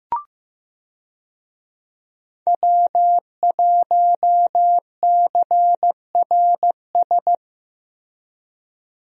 Morse code, heard as a keyed tone, about 15 wpm, 700 Hz. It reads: W1CRS
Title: USA Call Signs - 5 Characters - Repeated 3x - 15wpm